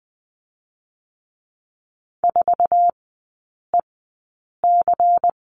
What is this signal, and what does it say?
Morse code, heard as a keyed tone, about 20 wpm, 700 Hz. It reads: 4 E C
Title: 178 Lesson 29e - Slash - Three Characters 20wpm